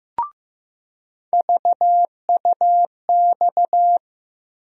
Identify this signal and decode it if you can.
Morse code, heard as a keyed tone, about 15 wpm, 700 Hz. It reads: VUX
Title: USA Call Sign Postfixes - 3 Letters 15wpm